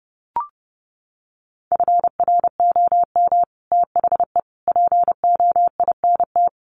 Morse code, heard as a keyed tone, about 30 wpm, 700 Hz. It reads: FROM THE POINT
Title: Sets of 5 Words 30wpm